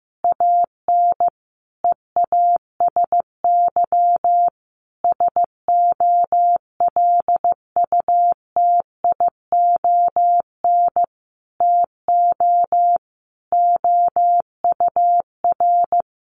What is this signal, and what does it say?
Morse code, heard as a keyed tone, about 14 wpm, 700 Hz. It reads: AN EASY SOLUTION TO OUR
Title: Sentences from Top 1300 Words 15wpm